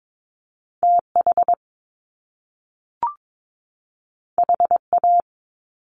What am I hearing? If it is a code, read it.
Morse code, heard as a keyed tone, about 22 wpm, 700 Hz. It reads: TH HA
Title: Top Two Letter Combinations in Words 22wpm